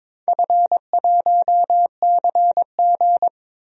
Morse code, heard as a keyed tone, about 22 wpm, 700 Hz. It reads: F1CG
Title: DXCC-100 Call Signs - 4 Characters - 22wpm